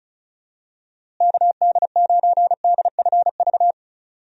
Morse code, heard as a keyed tone, about 34 wpm, 700 Hz. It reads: KD9DFV
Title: USA Call Signs - 6 Characters - Repeated 3x - 35wpm